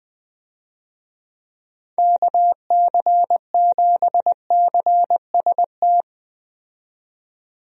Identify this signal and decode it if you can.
Morse code, heard as a keyed tone, about 20 wpm, 700 Hz. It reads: KC7CST
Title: USA Call Signs - 6 Characters - 20wpm